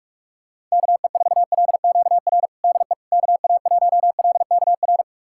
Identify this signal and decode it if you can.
Morse code, heard as a keyed tone, about 45 wpm, 700 Hz. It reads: KE4LQR DE KA1LKR
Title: USA Callsign DE Callsign - 6 Characters - 45wpm